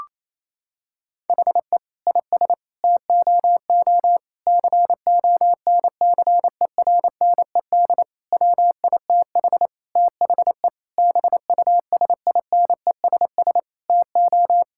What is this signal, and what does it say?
Morse code, heard as a keyed tone, about 28 wpm, 700 Hz. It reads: HE IS TOO CONCERNED WITH THE BUSINESS TO
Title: Sentences from Top 1000 Words - Review Part 6 - 28wpm